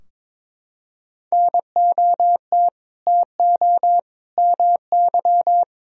Morse code, heard as a keyed tone, about 22 wpm, 700 Hz. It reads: NOT TO MY
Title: Sentences from Top 1000 Words - Review Part 4 - 22wpm